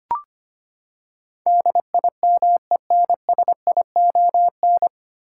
Morse code, heard as a keyed tone, about 25 wpm, 700 Hz. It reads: DIMENSION
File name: Nine Letter Words - 25wpm